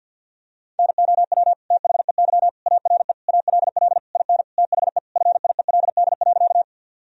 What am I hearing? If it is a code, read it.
Morse code, heard as a keyed tone, about 50 wpm, 700 Hz. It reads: NOW THEY ARE ALL IN THE FIELD.
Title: Sentences from Top 1000 Words - Review Part 2 - 50wpm